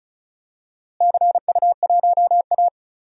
Morse code, heard as a keyed tone, about 34 wpm, 700 Hz. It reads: CU1A
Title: DXCC-100 Call Signs - 4 Characters - Repeated 2x - 35wpm